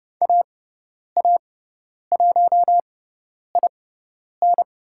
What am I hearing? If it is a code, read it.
Morse code, heard as a keyed tone, about 30 wpm, spaced as for 10 wpm, 700 Hz. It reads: AA1IN
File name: Lesson 16g - Letter c - Call Signs 30wpm